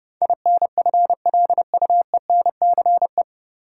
Morse code, heard as a keyed tone, about 30 wpm, 700 Hz. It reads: INFLUENCE
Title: Sentences from Top 1100 Words - New Word Drill Alt - 30wpm